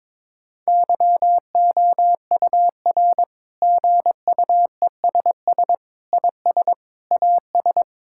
Morse code, heard as a keyed tone, about 22 wpm, 700 Hz. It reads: YOUR GUESS IS AS
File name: practice / 22wpm / Sets of 7 Words - 22wpm